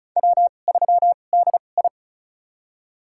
Morse code, heard as a keyed tone, about 36 wpm, spaced as for 25 wpm, 700 Hz. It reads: W3DI
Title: USA Call Signs - 2x Character Spacing - 35wpm